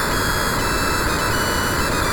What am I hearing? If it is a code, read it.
There is no code.